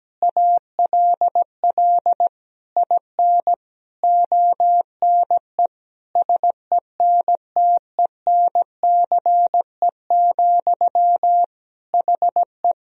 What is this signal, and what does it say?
Morse code, heard as a keyed tone, about 17 wpm, 700 Hz. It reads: ALL IN ONE SENTENCE, HE
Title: Sentences from Top 1000 Words - Review Part 1 - 17wpm